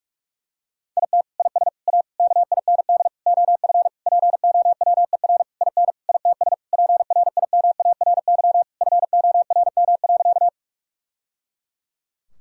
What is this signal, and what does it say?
Morse code, heard as a keyed tone, about 42 wpm, 700 Hz. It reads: IT IS A KIND OF POWER IN ITS PRIMARY FORM.
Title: Sentences from Top 1000 Words - Review Part 5 - 45wpm